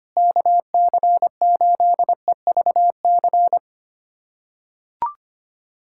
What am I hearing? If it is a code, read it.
Morse code, heard as a keyed tone, about 25 wpm, 700 Hz. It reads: KC8EVC
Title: USA Callsign DE Callsign 25wpm